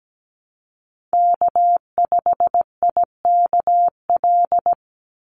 Morse code, heard as a keyed tone, about 17 wpm, 700 Hz. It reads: K5IKL
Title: USA Call Signs 17wpm